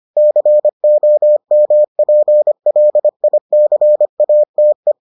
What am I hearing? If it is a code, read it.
Morse code, heard as a keyed tone, about 25 wpm, 600 Hz. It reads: COMPLICATE